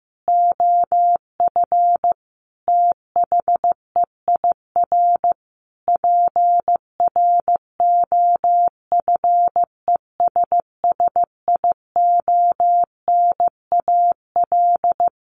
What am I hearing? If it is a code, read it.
Morse code, heard as a keyed tone, about 15 wpm, 700 Hz. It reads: OF THEIR PROFESSIONAL
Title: Sentences from Top 1400 Words 15wpm